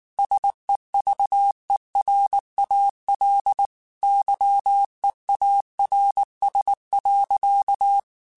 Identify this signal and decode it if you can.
Morse code, heard as a keyed tone, about 19 wpm, 800 Hz. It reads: SEVERAL YEARS.